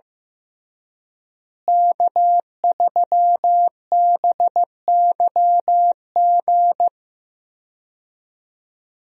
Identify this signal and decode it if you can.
Morse code, heard as a keyed tone, about 15 wpm, 700 Hz. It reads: K3BYG
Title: USA Call Signs - 5 Characters - Repeated 3x - 15wpm